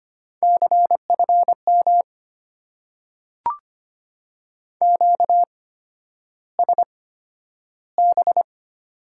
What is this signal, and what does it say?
Morse code, heard as a keyed tone, about 25 wpm, spaced as for 11 wpm, 700 Hz. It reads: CFM QSB
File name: Top 90 QSO Elements - 8x Character Spacing - 25wpm